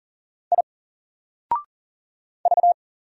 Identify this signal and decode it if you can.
Morse code, heard as a keyed tone, about 40 wpm, 700 Hz. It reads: I V
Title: Single Letter-Number 40wpm